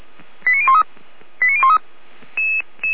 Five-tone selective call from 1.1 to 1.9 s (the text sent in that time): 89013